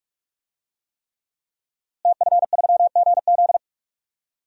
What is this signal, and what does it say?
Morse code, heard as a keyed tone, about 48 wpm, 700 Hz. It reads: TF3GZ
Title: DXCC-100 Call Signs - 5 Characters - Repeated 3x - 45wpm